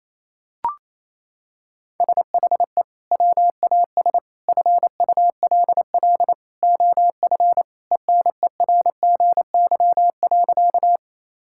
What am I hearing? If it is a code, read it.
Morse code, heard as a keyed tone, about 28 wpm, 700 Hz. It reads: SHE WAS FULL OF ENERGY.
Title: Sentences from Top 1000 Words - Review Part 6 - 28wpm